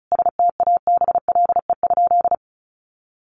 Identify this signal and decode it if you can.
Morse code, heard as a keyed tone, about 34 wpm, 700 Hz. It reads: STABLE?
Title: Sentences from Top 1500 Words 35wpm